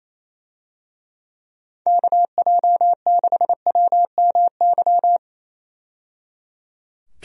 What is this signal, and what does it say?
Morse code, heard as a keyed tone, about 28 wpm, 700 Hz. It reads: KJ6WMY
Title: USA Call Signs - 6 Characters - 28wpm